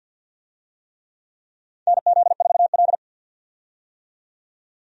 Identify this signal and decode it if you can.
Morse code, heard as a keyed tone, about 52 wpm, 700 Hz. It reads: NZ4L
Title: USA Call Signs - 4 Characters - 50wpm